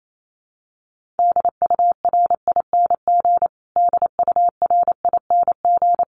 Morse code, heard as a keyed tone, about 28 wpm, 700 Hz. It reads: DURING DURING
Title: Top 500 Words Repeated 28wpm